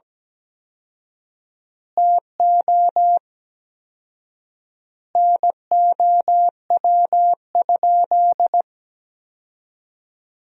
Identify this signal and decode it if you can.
Morse code, heard as a keyed tone, about 17 wpm, 700 Hz. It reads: TO NOW?
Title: 4x Word-Spacing - Sentences from Top 100 Words - 17wpm